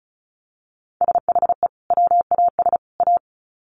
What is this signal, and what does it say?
Morse code, heard as a keyed tone, about 34 wpm, 700 Hz. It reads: SHE WAS A
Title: Sets of 3 Words 35wpm